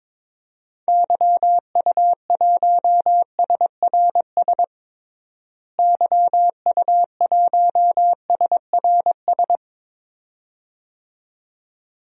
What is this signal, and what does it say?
Morse code, heard as a keyed tone, about 22 wpm, 700 Hz. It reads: YU1SRS YU1SRS
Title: DXCC-100 Call Signs - 6 Characters - Repeated 2x - 22wpm